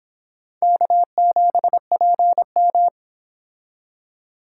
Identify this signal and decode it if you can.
Morse code, heard as a keyed tone, about 25 wpm, 700 Hz. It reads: K7PM